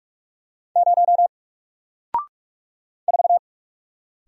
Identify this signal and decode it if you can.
Morse code, heard as a keyed tone, about 43 wpm, 700 Hz. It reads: 0 4
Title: Single Character (Letter:Number:Punctuation:Prosign) Alt 45wpm